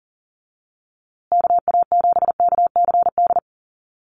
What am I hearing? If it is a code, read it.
Morse code, heard as a keyed tone, about 40 wpm, 700 Hz. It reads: KA7KCD
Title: USA Call Signs Repeated 3x 40wpm